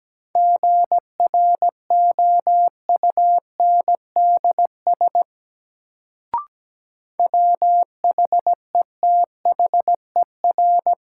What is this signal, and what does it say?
Morse code, heard as a keyed tone, about 17 wpm, 700 Hz. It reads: GROUNDS WHETHER
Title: Sentences from Top 1300 Words - New Word Drill Alt - 17wpm